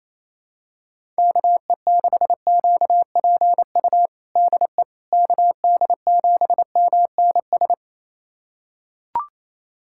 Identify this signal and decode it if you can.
Morse code, heard as a keyed tone, about 28 wpm, 700 Hz. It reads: KE6QPU DE KD7MNS
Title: USA Callsign DE Callsign 28wpm